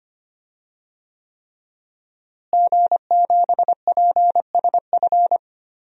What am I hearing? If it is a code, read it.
Morse code, heard as a keyed tone, about 25 wpm, 700 Hz. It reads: G7PSF
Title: DXCC-100 Call Signs - 5 Characters - Repeated 3x - 25wpm